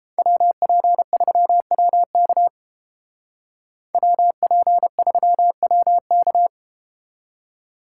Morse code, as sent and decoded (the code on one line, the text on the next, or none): .-- .--. ...-- .-- -.- / .-- .--. ...-- .-- -.-
WP3WK WP3WK